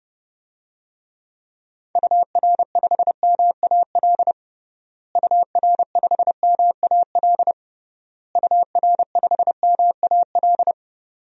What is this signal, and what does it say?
Morse code, heard as a keyed tone, about 32 wpm, 700 Hz. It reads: UR5MAL UR5MAL UR5MAL
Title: DXCC-100 Call Signs - 6 Characters - Repeated 3x - 30wpm